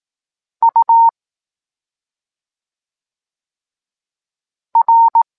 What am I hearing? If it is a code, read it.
Morse code, heard as a keyed tone, about 18 wpm, 920 Hz. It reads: U R
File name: Tekst-26 1